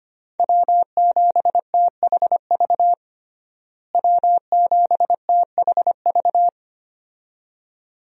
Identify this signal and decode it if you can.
Morse code, heard as a keyed tone, about 25 wpm, 700 Hz. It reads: W7THV W7THV
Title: USA Call Signs - 5 Characters - Repeated 2x - 25wpm